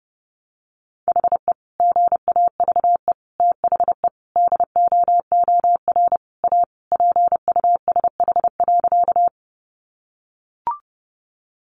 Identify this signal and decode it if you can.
Morse code, heard as a keyed tone, about 30 wpm, 700 Hz. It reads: HE GAVE THE DOOR A PUSH.
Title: Sentences from Top 1300 Words 30wpm